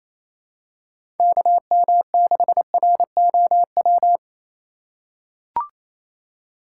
Morse code, heard as a keyed tone, about 28 wpm, 700 Hz. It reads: KM6ROW
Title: USA Call Signs - 6 Characters - 3x Character Spacing - 28wpm